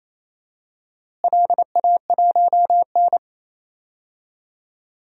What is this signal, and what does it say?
Morse code, heard as a keyed tone, about 28 wpm, 700 Hz. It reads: LA1N
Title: DXCC-100 Call Signs - 4 Characters - Repeated 2x - 28wpm